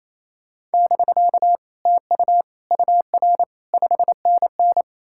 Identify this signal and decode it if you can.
Morse code, heard as a keyed tone, about 36 wpm, 700 Hz. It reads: <BK> TU UR 5NN